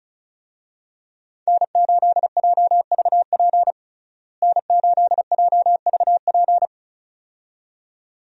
Morse code, heard as a keyed tone, about 36 wpm, 700 Hz. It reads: N8JVP N8JVP
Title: USA Call Signs - 5 Characters - Repeated 2x - 35wpm